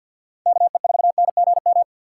Morse code, heard as a keyed tone, about 48 wpm, 700 Hz. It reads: KE4NGM
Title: USA Call Signs - 4x Character Spacing - 50wpm